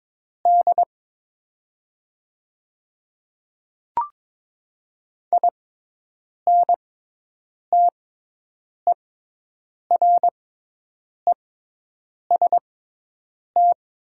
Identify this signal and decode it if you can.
Morse code, heard as a keyed tone, about 22 wpm, spaced as for 8 wpm, 700 Hz. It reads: D INTEREST
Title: 6x Character Spacing - Top 500 Words - 22wpm